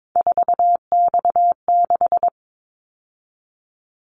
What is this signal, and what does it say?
Morse code, heard as a keyed tone, about 22 wpm, 700 Hz. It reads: 4X6
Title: Callsign Prefixes for Top 50 DXCC - 22wpm